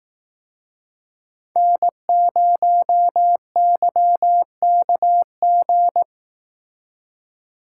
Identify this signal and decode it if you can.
Morse code, heard as a keyed tone, about 18 wpm, 700 Hz. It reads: N0YKG
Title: Speed Racing - USA 1x3 Call Signs 15wpm